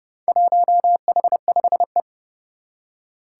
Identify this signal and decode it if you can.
Morse code, heard as a keyed tone, about 29 wpm, 700 Hz. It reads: 1H5E
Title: Four Challenging Letter-Number Combinations 30wpm